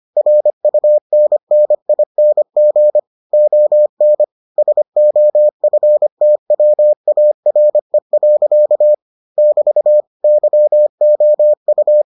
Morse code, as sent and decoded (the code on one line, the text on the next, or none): .-. ..- -. -. .. -. --. / --- -. / ... --- ..-. - .-- .- .-. .-.-.- / -...- / -.-- --- ..-
RUNNING ON SOFTWARE. <BT> YOU